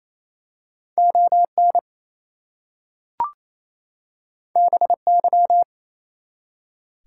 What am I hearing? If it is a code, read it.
Morse code, heard as a keyed tone, about 27 wpm, 700 Hz. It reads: ON BY
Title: Two Letter Words - 28wpm